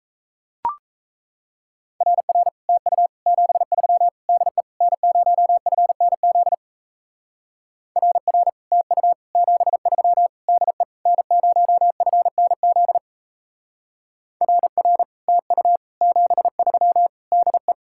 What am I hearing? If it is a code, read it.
Morse code, heard as a keyed tone, about 36 wpm, 700 Hz. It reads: RR TU 73 DE N0FNZ RR TU 73 DE N0FNZ RR TU 73 DE